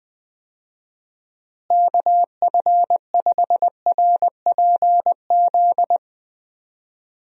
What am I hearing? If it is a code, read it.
Morse code, heard as a keyed tone, about 20 wpm, 700 Hz. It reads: KF5RPZ
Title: USA Call Signs - 6 Characters - Repeated 2x - 20wpm